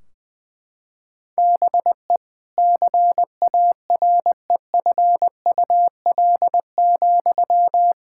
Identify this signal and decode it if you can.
Morse code, heard as a keyed tone, about 19 wpm, 700 Hz. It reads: BE CAREFUL,
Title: Sentences from Top 1600 Words 20wpm